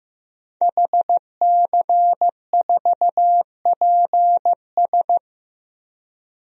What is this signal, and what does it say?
Morse code, heard as a keyed tone, about 15 wpm, 700 Hz. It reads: HC4PS
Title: DXCC-100 Call Signs - 5 Characters - Repeated 3x - 15wpm